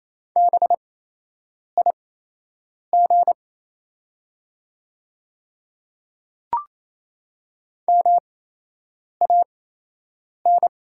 Morse code, heard as a keyed tone, about 28 wpm, spaced as for 8 wpm, 700 Hz. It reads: BIG MAN
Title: 8x Character Spacing - Top 200 Words - 28wpm